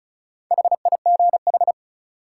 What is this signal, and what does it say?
Morse code, heard as a keyed tone, about 36 wpm, 700 Hz. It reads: HIGH
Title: Sentences from Top 200 Words - New Word Drill - 35wpm